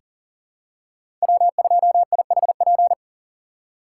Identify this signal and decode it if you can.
Morse code, heard as a keyed tone, about 40 wpm, 700 Hz. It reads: W2IHP
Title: USA Call Signs - 5 Characters - 4x Character Spacing - 40wpm